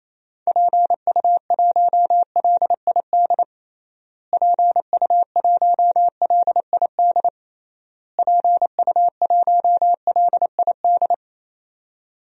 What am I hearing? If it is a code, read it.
Morse code, heard as a keyed tone, about 28 wpm, 700 Hz. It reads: PU1LID PU1LID PU1LID
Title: DXCC-100 Call Signs - 6 Characters - Repeated 3x - 28wpm